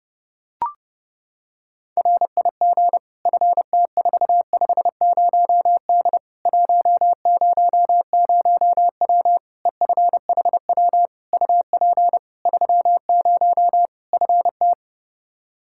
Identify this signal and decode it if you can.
Morse code, heard as a keyed tone, about 30 wpm, 700 Hz. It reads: RIG FT450D 100W EFHW UP 30 FT
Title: K1USN SST QSO Practice 30wpm